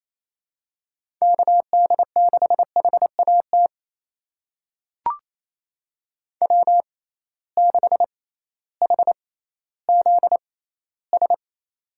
Morse code, heard as a keyed tone, about 28 wpm, spaced as for 15 wpm, 700 Hz. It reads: KD6HAT W6HZS
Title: USA Call Signs - 6x Character Spacing - 28wpm